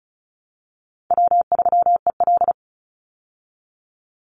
Morse code, heard as a keyed tone, about 34 wpm, 700 Hz. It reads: W3EL
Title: Easy Call Signs Repeated 3x - 35wpm